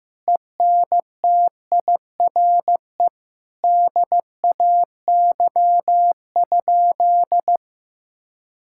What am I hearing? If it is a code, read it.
Morse code, heard as a keyed tone, about 15 wpm, 700 Hz. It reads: ENTIRE DAY?
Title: Sentences from Top 1000 Words - Review Part 5 - 15wpm